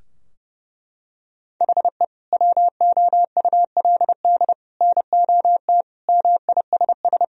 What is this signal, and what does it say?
Morse code, heard as a keyed tone, about 30 wpm, 700 Hz. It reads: HE WOULD NOT MISS
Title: Sentences from Top 1000 Words - Review Part 3 - 30wpm